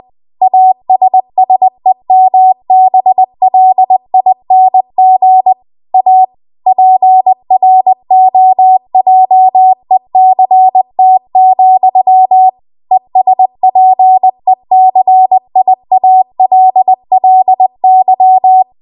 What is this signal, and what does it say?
Morse code, heard as a keyed tone, about 20 wpm, 750 Hz. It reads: ASSEMBLING A PROJECT, ESPECIALLY
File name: Archive / 20 WPM / 20WPM